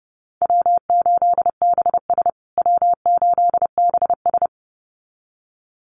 Morse code, heard as a keyed tone, about 30 wpm, 700 Hz. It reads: W8BS W8BS
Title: USA Call Signs Repeated 2x 30wpm